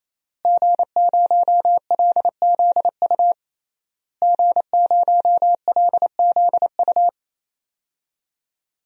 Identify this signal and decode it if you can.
Morse code, heard as a keyed tone, about 28 wpm, 700 Hz. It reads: G0LZU G0LZU
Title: DXCC-100 Call Signs - 5 Characters - Repeated 3x - 28wpm